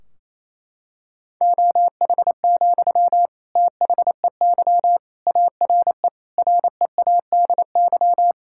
Morse code, heard as a keyed tone, about 28 wpm, 700 Hz. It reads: OH, THEY ARE READY
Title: Sentences from Top 1600 Words 28wpm